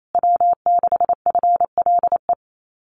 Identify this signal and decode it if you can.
Morse code, heard as a keyed tone, about 28 wpm, 700 Hz. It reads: W6FLE
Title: Common QSO Phrases Encore 28wpm